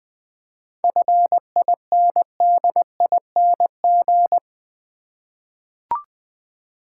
Morse code, heard as a keyed tone, about 20 wpm, 700 Hz. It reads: FINDING
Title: Sentences from Top 1100 Words - New Word Drill Alt v2 - 20wpm